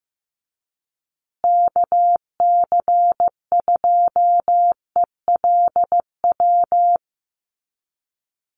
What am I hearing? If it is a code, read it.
Morse code, heard as a keyed tone, about 15 wpm, 700 Hz. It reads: KC2ELW
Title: USA Call Signs Repeated 2x 15wpm